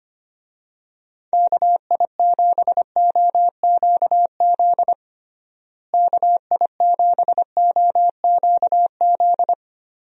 Morse code, heard as a keyed tone, about 25 wpm, 700 Hz. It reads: KI7OQZ KI7OQZ
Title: USA Call Signs - 6 Characters - Repeated 3x - 25wpm